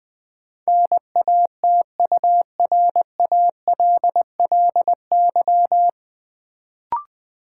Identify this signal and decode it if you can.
Morse code, heard as a keyed tone, about 20 wpm, 700 Hz. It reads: NATURALLY
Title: Sentences from Top 1300 Words - New Word Drill - 20wpm